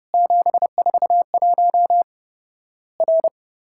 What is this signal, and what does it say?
Morse code, heard as a keyed tone, about 30 wpm, 700 Hz. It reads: 741 R
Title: Worked All Europe DX Contest - QTC Exchange - 30wpm